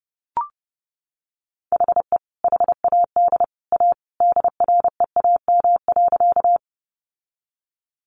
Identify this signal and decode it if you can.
Morse code, heard as a keyed tone, about 30 wpm, 700 Hz. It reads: HE HAD A DREAM.
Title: Sentences from Top 1500 Words 30wpm